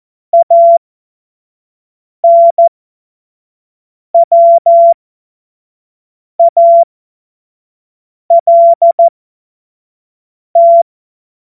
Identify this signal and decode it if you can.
Morse code, heard as a keyed tone, about 14 wpm, spaced as for 5 wpm, 680 Hz. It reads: ANWALT